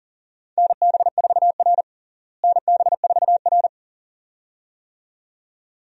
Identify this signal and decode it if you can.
Morse code, heard as a keyed tone, about 40 wpm, 700 Hz. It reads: NB4R NB4R
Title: USA Call Signs - 4 Characters - Repeated 2x - 40wpm